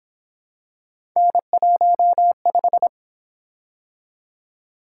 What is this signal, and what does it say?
Morse code, heard as a keyed tone, about 26 wpm, 700 Hz. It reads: N15
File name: Speed-Racing - Three Letter-Number Combinations - 22wpm